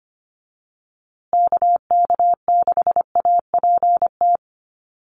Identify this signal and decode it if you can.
Morse code, heard as a keyed tone, about 25 wpm, 700 Hz. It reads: KK6APT
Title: USA Call Signs Repeated 2x 25wpm